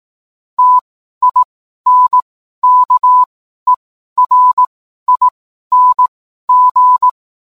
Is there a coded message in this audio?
Yes: Morse code from 0.6 to 7.1 s, heard as a keyed tone, about 18 wpm, spaced as for 12 wpm, 1000 Hz. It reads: TINKERING